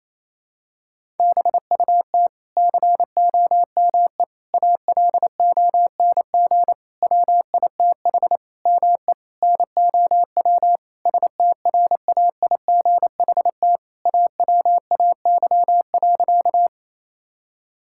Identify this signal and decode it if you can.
Morse code, heard as a keyed tone, about 28 wpm, 700 Hz. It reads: BUT COME ALONG WITH ME NOW STRAIGHT AWAY.
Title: Sentences from Top 1000 Words - Review Part 6 - 28wpm